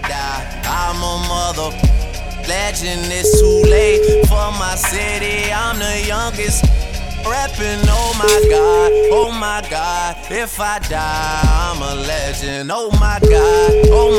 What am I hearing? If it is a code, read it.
Ringback tone (Europe, 425 Hz), with noise.